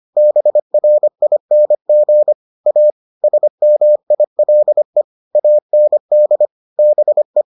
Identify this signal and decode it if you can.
Morse code, heard as a keyed tone, about 25 wpm, 600 Hz. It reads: BRING A SMILE AND BE